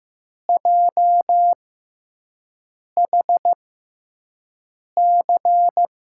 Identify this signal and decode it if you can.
Morse code, heard as a keyed tone, about 15 wpm, spaced as for 5 wpm, 700 Hz. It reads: JHC
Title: USA Call Signs - 5 Characters - 6x Character Spacing - 15wpm